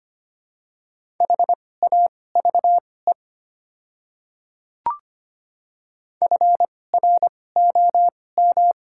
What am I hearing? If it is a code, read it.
Morse code, heard as a keyed tone, about 25 wpm, spaced as for 18 wpm, 700 Hz. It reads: HAVE FROM
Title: 2x Character Spacing - Top 100 Words - 25wpm